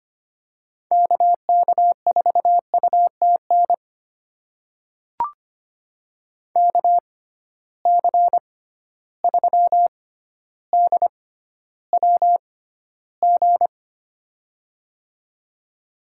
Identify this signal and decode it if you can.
Morse code, heard as a keyed tone, about 25 wpm, spaced as for 13 wpm, 700 Hz. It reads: KK4UTN KC3DWG
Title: USA Call Signs - 6 Characters - 6x Character Spacing - 25wpm